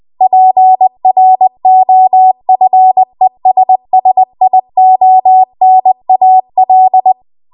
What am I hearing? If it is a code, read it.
Morse code, heard as a keyed tone, about 20 wpm, 750 Hz. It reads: PROFESSIONAL
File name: Archive / 20 WPM / 20WPM